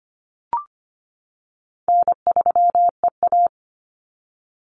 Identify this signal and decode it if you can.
Morse code, heard as a keyed tone, about 25 wpm, 700 Hz. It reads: N3EA
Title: Easy Call Signs - 25wpm